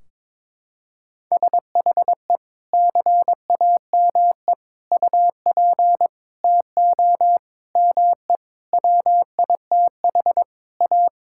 Morse code, heard as a keyed tone, about 22 wpm, 700 Hz. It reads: SHE CAME UP TO ME WITH A
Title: Sentences from Top 1600 Words 22wpm